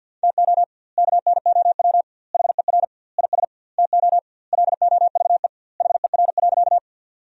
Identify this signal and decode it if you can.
Morse code, heard as a keyed tone, about 50 wpm, 700 Hz. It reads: TO KNOW HER IS TO LOVE HER.
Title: 2x Word-Spacing - Sentences from Top 400 Words - 50wpm